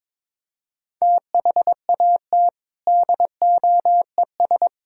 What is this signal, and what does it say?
Morse code, heard as a keyed tone, about 21 wpm, 700 Hz. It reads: THAT DOES
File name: practice / 22wpm / Sentences with 3-Word Phrases - 22wpm